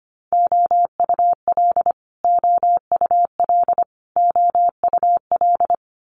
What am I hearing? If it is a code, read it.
Morse code, heard as a keyed tone, about 25 wpm, 700 Hz. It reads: OUL OUL OUL
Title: Multi-letter Phonemes Repeated 3x 25wpm